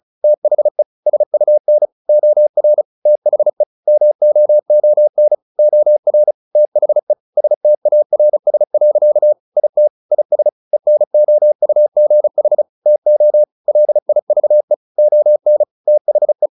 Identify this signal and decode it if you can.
Morse code, heard as a keyed tone, about 34 wpm, 600 Hz. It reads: THE SUN OR THE MOON OR THE STARS. IT IS ENOUGH TO LIVE ON THE
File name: the old man and the sea 2 35wpm